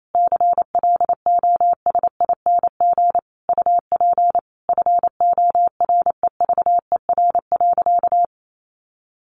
Morse code, heard as a keyed tone, about 28 wpm, 700 Hz. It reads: CLOSING UP FOREVER.
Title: A Sound Of Thunder 28wpm